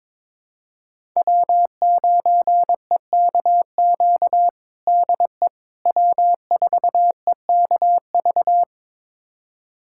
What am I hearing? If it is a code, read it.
Morse code, heard as a keyed tone, about 22 wpm, 700 Hz. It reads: W9EKQ DE W4EKV
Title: USA Callsign DE Callsign - 5 Characters - 22wpm